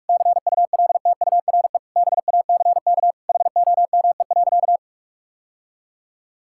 Morse code, heard as a keyed tone, about 46 wpm, 700 Hz. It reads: CULTURE BACK HOME.
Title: Sentences from Top 1000 Words - Review Part 6 - 45wpm